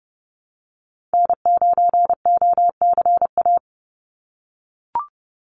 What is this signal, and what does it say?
Morse code, heard as a keyed tone, about 30 wpm, 700 Hz. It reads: N9OCA
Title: USA Call Signs 30wpm